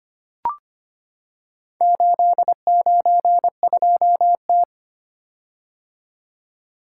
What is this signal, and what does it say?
Morse code, heard as a keyed tone, about 25 wpm, 700 Hz. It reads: 892T
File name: practice / 25wpm / Four Letter-Number Combinations - 25wpm